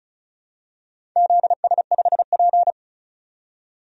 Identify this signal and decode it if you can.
Morse code, heard as a keyed tone, about 36 wpm, 700 Hz. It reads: ZS5P
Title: DXCC-100 Call Signs - 4 Characters - 35wpm